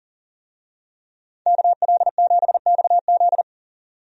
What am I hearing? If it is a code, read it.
Morse code, heard as a keyed tone, about 40 wpm, 700 Hz. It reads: KL7XZ
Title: DXCC-100 Call Signs - 5 Characters - Repeated 3x - 40wpm